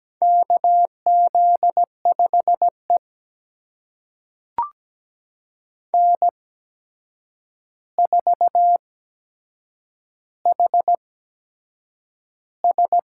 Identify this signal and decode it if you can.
Morse code, heard as a keyed tone, about 17 wpm, spaced as for 7 wpm, 700 Hz. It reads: KZ5E N4HS
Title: USA Call Signs - 4 Characters - 8x Character Spacing - 17wpm